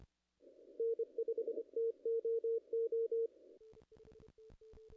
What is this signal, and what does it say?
Morse code, heard as a keyed tone, about 26 wpm, 440 Hz. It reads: N5TOO
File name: contests / ss-cw-k5tr-op-n5rz -0100